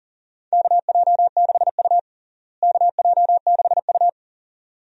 Morse code, heard as a keyed tone, about 40 wpm, 700 Hz. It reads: KJ6U KJ6U
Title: USA Call Signs - 4 Characters - Repeated 2x - 40wpm